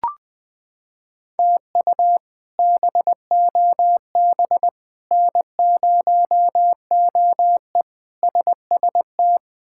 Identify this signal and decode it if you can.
Morse code, heard as a keyed tone, about 20 wpm, 700 Hz. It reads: TU BOB N0OE SST
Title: K1USN SST QSO Practice - 20wpm